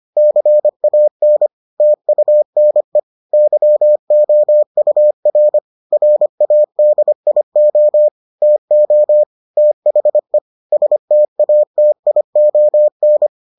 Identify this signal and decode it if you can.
Morse code, heard as a keyed tone, about 25 wpm, 600 Hz. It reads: CAN TUNE YOUR RADIO TO THE STATION